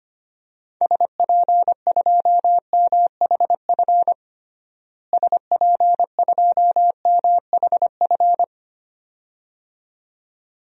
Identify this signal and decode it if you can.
Morse code, heard as a keyed tone, about 25 wpm, 700 Hz. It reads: SP2MHF SP2MHF
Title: DXCC-100 Call Signs - 6 Characters - Repeated 2x - 25wpm